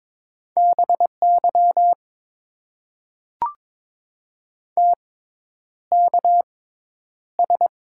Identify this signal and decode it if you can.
Morse code, heard as a keyed tone, about 21 wpm, spaced as for 10 wpm, 700 Hz. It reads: BY TKS
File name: Top 90 QSO Elements - 6x Character Spacing - 22wpm